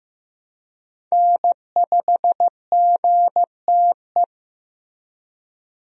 Morse code, heard as a keyed tone, about 15 wpm, 700 Hz. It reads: N5GTE
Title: USA Call Signs - 5 Characters - Repeated 3x - 15wpm